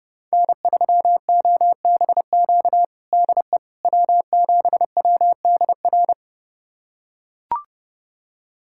Morse code, heard as a keyed tone, about 30 wpm, 700 Hz. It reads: N3OBQ DE W7WDR
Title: USA Callsign DE Callsign - 5 Characters - 30wpm